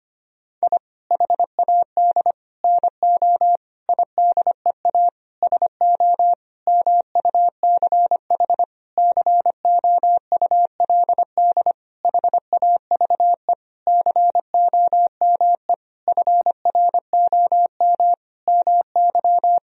Morse code, heard as a keyed tone, about 25 wpm, 700 Hz. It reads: I HAD NO IDEA SO MUCH COULD HAVE COME FROM MY
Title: Sentences from Top 1000 Words - Review Part 4 - 25wpm